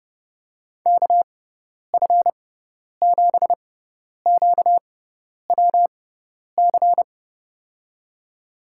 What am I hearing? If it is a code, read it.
Morse code, heard as a keyed tone, about 32 wpm, spaced as for 10 wpm, 700 Hz. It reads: KF7QWC
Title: USA Call Signs - 6 Characters - 6x Character Spacing - 30wpm